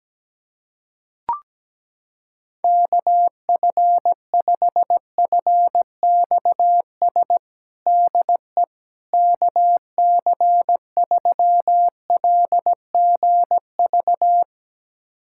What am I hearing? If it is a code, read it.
Morse code, heard as a keyed tone, about 17 wpm, 700 Hz. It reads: KF5FXS DE KC3LGV
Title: USA Callsign DE Callsign - 6 Characters - 17wpm